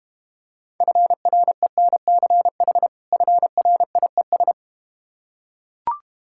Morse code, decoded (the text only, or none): FRENCH FRIES